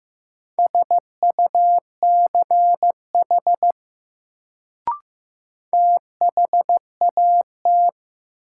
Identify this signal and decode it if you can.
Morse code, heard as a keyed tone, about 15 wpm, 700 Hz. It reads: SUCH THAT
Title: Sets of 2 Words - New Word Drill - 15wpm